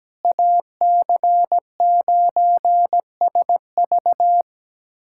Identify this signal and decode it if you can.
Morse code, heard as a keyed tone, about 17 wpm, 700 Hz. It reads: AC9SV
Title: USA Callsign DE Callsign - 5 Characters - 17wpm